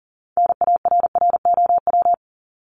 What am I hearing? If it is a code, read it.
Morse code, heard as a keyed tone, about 39 wpm, 700 Hz. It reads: NARROW
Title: Top 1501-1750 Words 40wpm